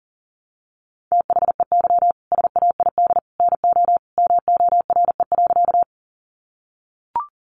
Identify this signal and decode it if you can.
Morse code, heard as a keyed tone, about 40 wpm, 700 Hz. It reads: THEY SAID NO MORE.
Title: Sentences from Top 100 Words 40wpm